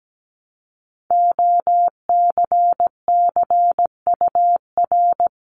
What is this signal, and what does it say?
Morse code, heard as a keyed tone, about 17 wpm, 700 Hz. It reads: OCCUR
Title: Multi-letter Phonemes 17wpm